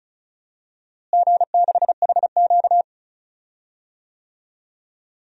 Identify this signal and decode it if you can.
Morse code, heard as a keyed tone, about 34 wpm, 700 Hz. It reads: G6HQ
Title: DXCC-100 Call Signs - 4 Characters - Repeated 3x - 35wpm